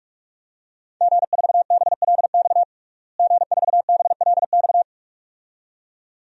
Morse code, heard as a keyed tone, about 42 wpm, 700 Hz. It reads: G4BLX G4BLX
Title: DXCC-100 Call Signs - 5 Characters - Repeated 2x - 45wpm